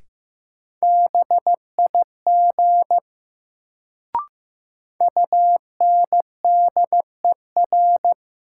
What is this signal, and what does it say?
Morse code, heard as a keyed tone, about 15 wpm, 700 Hz. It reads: BIG UNDER
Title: Sets of 2 Words - New Word Drill - 15wpm